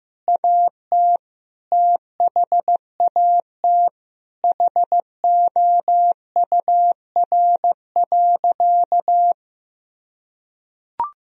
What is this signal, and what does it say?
Morse code, heard as a keyed tone, about 15 wpm, 700 Hz. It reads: AT THAT HOUR.
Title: Sentences from Top 1000 Words - Review Part 3 - 15wpm